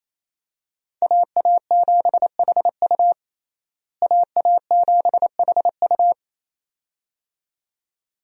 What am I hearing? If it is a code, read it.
Morse code, heard as a keyed tone, about 28 wpm, 700 Hz. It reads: AA7HU AA7HU
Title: USA Call Signs - 5 Characters - Repeated 2x - 28wpm